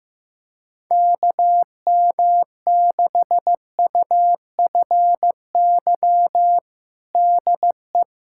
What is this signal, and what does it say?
Morse code, heard as a keyed tone, about 15 wpm, 700 Hz. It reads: KM6UFY DE
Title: USA Callsign DE Callsign 15wpm